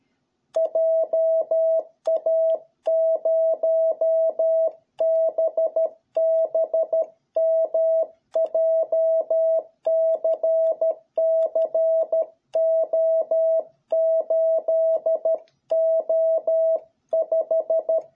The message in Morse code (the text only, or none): JA0BBM JCCO8O5